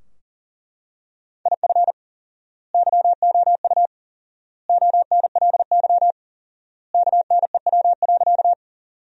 Morse code, heard as a keyed tone, about 40 wpm, 700 Hz. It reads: IF YOU ONLY KNEW.
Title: 4x Word-Spacing - Sentences from Top 400 Words - 40wpm